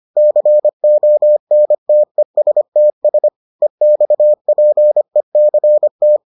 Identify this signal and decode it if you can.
Morse code, heard as a keyed tone, about 25 wpm, 600 Hz. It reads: CONTESTS EXPECT